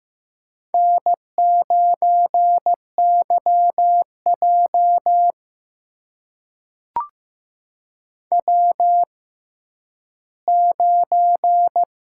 Morse code, heard as a keyed tone, about 15 wpm, spaced as for 10 wpm, 700 Hz. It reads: N9YJ W9
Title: USA Call Signs - 6x Character Spacing - 15wpm